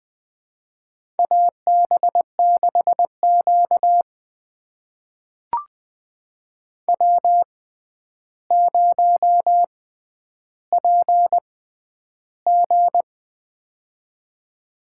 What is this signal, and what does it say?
Morse code, heard as a keyed tone, about 20 wpm, spaced as for 10 wpm, 700 Hz. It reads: AB6Q W0PG
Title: USA Call Signs - 4 Characters - 6x Character Spacing - 20wpm